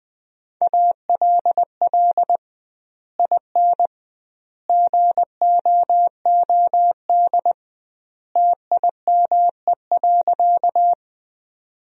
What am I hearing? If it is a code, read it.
Morse code, heard as a keyed tone, about 21 wpm, 700 Hz. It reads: ALL IN GOOD TIME.
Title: 2x Word-Spacing - Sentences from Top 200 Words - 20wpm